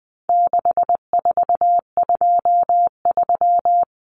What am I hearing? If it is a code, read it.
Morse code, heard as a keyed tone, about 20 wpm, 700 Hz. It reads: 6423
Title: Numbers - 4-Digit 20wpm